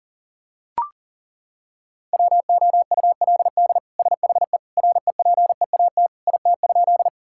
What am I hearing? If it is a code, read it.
Morse code, heard as a keyed tone, about 40 wpm, 700 Hz. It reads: WOULD SHE REPEAT IT?
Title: Sentences from Top 1600 Words 40wpm